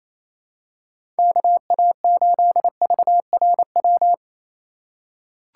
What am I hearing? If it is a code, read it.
Morse code, heard as a keyed tone, about 27 wpm, 700 Hz. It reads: KA8VRW
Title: USA Call Signs - 2x Character Spacing - 28wpm